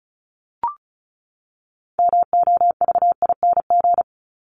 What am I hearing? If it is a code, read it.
Morse code, heard as a keyed tone, about 34 wpm, 700 Hz. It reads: MOVING
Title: ING Suffix - Top 100 Words with Base Words 35wpm